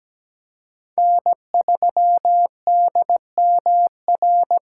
Morse code, heard as a keyed tone, about 20 wpm, 700 Hz. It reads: N3DMR
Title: Speed Racing - USA 1x3 Call Signs 17wpm